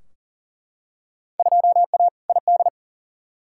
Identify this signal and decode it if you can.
Morse code, heard as a keyed tone, about 40 wpm, 700 Hz. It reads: 2A ID
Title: Field Day Exchanges - 40wpm